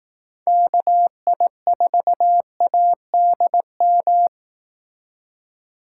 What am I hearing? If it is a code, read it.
Morse code, heard as a keyed tone, about 18 wpm, 700 Hz. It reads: KI4ADM
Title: Speed Racing - USA 2x3 Call Signs 15wpm